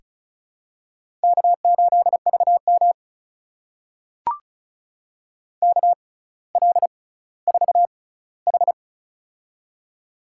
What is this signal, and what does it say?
Morse code, heard as a keyed tone, about 36 wpm, spaced as for 18 wpm, 700 Hz. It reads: K8VM KL4H
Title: USA Call Signs - 4 Characters - 6x Character Spacing - 35wpm